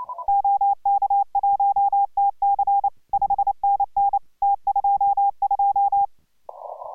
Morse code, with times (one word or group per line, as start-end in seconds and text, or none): 0.3-2.9 OK1TC
3.1-4.2 5NN
4.4-6.1 T22